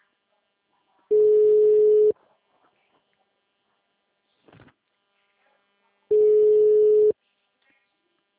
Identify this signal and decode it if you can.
Ringback tone (Europe, 425 Hz).
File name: out- -2003- - - .442